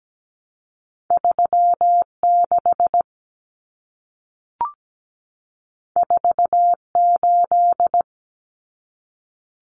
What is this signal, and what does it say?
Morse code, heard as a keyed tone, about 17 wpm, 700 Hz. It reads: 36 48
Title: Common Cut Numbers - 2-Digit 17wpm